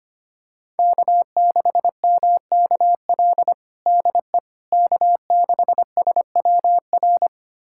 Morse code, heard as a keyed tone, about 25 wpm, 700 Hz. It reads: K6MKL DE K6SWR
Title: USA Callsign DE Callsign - 5 Characters - 25wpm